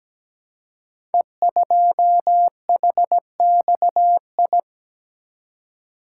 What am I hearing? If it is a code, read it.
Morse code, heard as a keyed tone, about 17 wpm, 700 Hz. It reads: E2HXI
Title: Code Groups - 5 Letter:Numbers - 17wpm